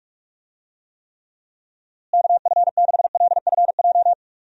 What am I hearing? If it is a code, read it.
Morse code, heard as a keyed tone, about 41 wpm, 700 Hz. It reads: KF6LFJ
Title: USA Call Signs - 6 Characters - Repeated 2x - 45wpm